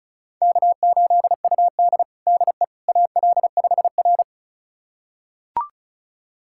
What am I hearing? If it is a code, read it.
Morse code, heard as a keyed tone, about 36 wpm, 700 Hz. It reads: K8UD DE AL5R
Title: USA Callsign DE Callsign - 4 Characters - 35wpm